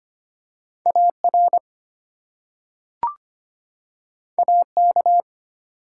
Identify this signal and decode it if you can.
Morse code, heard as a keyed tone, about 25 wpm, 700 Hz. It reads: AR AK
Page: morse-code-ninja-practice-25wpm